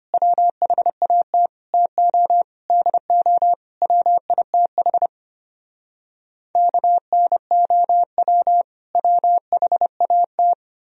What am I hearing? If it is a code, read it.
Morse code, heard as a keyed tone, about 33 wpm, 700 Hz. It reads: WHAT TO DO WITH KNOW WHAT